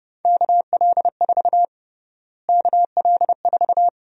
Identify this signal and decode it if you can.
Morse code, heard as a keyed tone, about 32 wpm, 700 Hz. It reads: KL4 KL4
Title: USA Call Sign Prefixes - Repeated 2x - 30wpm